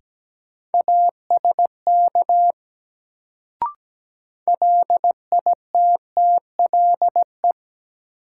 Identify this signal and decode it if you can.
Morse code, heard as a keyed tone, about 17 wpm, 700 Hz. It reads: ASK LITTLE
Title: Sentences from Top 200 Words - New Word Drill - 17wpm